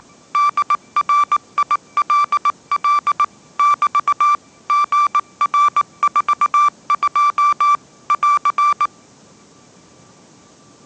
Morse code, heard as a keyed tone, about 19 wpm, 1170 Hz. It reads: DRILL <BT> GR42 <AR>
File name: 20wpm